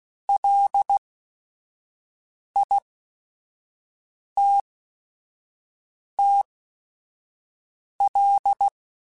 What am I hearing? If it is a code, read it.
Morse code, heard as a keyed tone, about 16 wpm, spaced as for 5 wpm, 800 Hz. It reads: LITTL